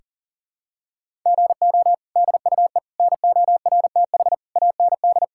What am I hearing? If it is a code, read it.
Morse code, heard as a keyed tone, about 37 wpm, 700 Hz. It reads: GO DUE NORTH AND
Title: Sentences from Top 1000 Words - Review Part 6 - 40wpm